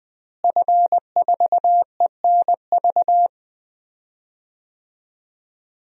Morse code, heard as a keyed tone, about 20 wpm, 700 Hz. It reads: F4ENV
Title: DXCC-100 Call Signs - 5 Characters - Repeated 2x - 20wpm